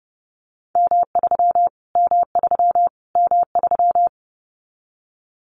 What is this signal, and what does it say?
Morse code, heard as a keyed tone, about 30 wpm, 700 Hz. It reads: M3 M3 M3
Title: Callsign Prefixes for Top 25 DXCC - Repeated 3x - 30wpm